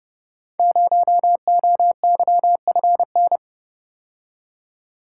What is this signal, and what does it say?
Morse code, heard as a keyed tone, about 30 wpm, 700 Hz. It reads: 0OYFN
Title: Code Groups - 5 Letter:Numbers - 30wpm